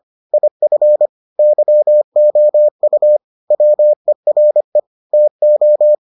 Morse code, heard as a keyed tone, about 25 wpm, 600 Hz. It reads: IF YOU WERE TO